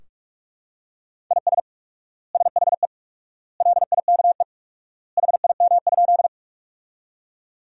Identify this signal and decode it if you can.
Morse code, heard as a keyed tone, about 46 wpm, 700 Hz. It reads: IS SHE LIKE HIM?
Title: 4x Word-Spacing - Sentences from Top 100 Words - 45wpm